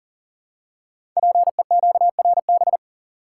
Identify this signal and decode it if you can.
Morse code, heard as a keyed tone, about 41 wpm, 700 Hz. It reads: PEQRB
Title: Code Groups - 5 Letters - 40wpm